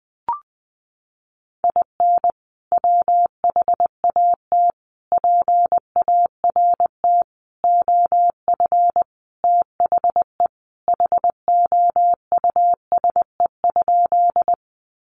Morse code, heard as a keyed tone, about 20 wpm, 700 Hz. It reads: IN WHAT PART OF THE HOUSE?
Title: Sentences from Top 200 Words 20wpm